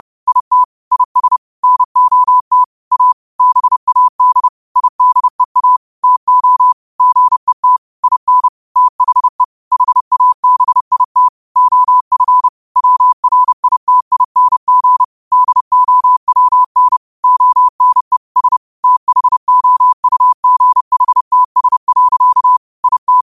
Morse code, heard as a keyed tone, about 30 wpm, 1000 Hz. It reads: IT IS NOT A BAD IDEA TO GET IN THE HABIT OF WRITING DOWN ONE S THOUGHTS. IT